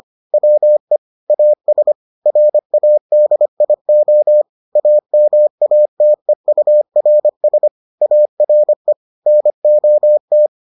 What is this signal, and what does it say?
Morse code, heard as a keyed tone, about 25 wpm, 600 Hz. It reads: WE AS RADIO AMATEURS ARE NOT